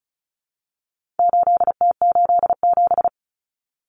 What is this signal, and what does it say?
Morse code, heard as a keyed tone, about 34 wpm, 700 Hz. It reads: 8T87
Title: Common Cut Numbers - 4-Digit 35wpm